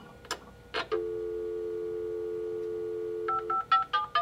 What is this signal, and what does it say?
Dial tone (North America), 350 + 440 Hz.